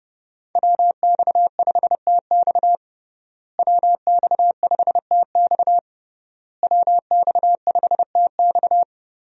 Morse code, heard as a keyed tone, about 32 wpm, 700 Hz. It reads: WX5TX WX5TX WX5TX
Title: USA Call Signs - 5 Characters - Repeated 3x - 30wpm